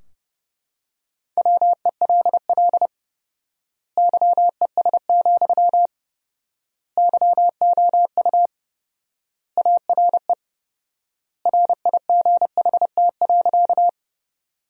Morse code, heard as a keyed tone, about 30 wpm, 700 Hz. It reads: WELL YES, YOU ARE RIGHT.
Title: 4x Word-Spacing - Sentences from Top 500 Words - 30wpm